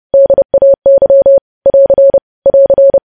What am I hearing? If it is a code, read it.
Morse code, heard as a keyed tone, about 30 wpm, 550 Hz. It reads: DAY <AR> <AR>
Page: news headlines in morse code at 30 wpm